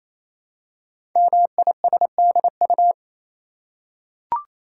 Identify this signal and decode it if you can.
Morse code, heard as a keyed tone, about 27 wpm, 700 Hz. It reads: MISDU
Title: Code Groups - 5 Letters - 28wpm